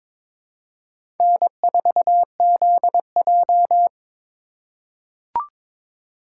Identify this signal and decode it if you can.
Morse code, heard as a keyed tone, about 22 wpm, 700 Hz. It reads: N4ZJ
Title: USA Call Signs - 4 Characters - 3x Character Spacing - 22wpm